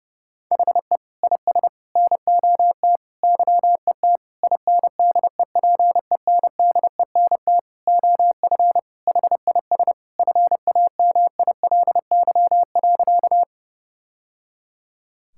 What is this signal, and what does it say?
Morse code, heard as a keyed tone, about 30 wpm, 700 Hz. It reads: HE IS NOT YET INDEPENDENT OF HIS FAMILY.
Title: Sentences from Top 1000 Words - Review Part 6 - 30wpm